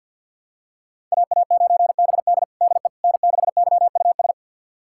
Morse code, heard as a keyed tone, about 51 wpm, 700 Hz. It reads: AA9BD DE N6YUS
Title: USA Callsign DE Callsign 50wpm